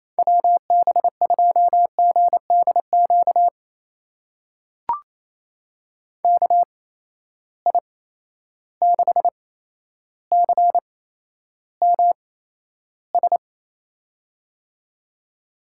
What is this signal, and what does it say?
Morse code, heard as a keyed tone, about 28 wpm, spaced as for 12 wpm, 700 Hz. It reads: WB2GDQ KI6CMS
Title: USA Call Signs - 6 Characters - 8x Character Spacing - 28wpm